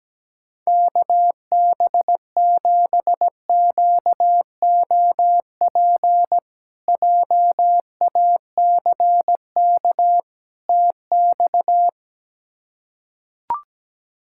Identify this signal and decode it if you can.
Morse code, heard as a keyed tone, about 17 wpm, 700 Hz. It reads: KB7QOP JACK TX
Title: K1USN SST QSO Practice - 17wpm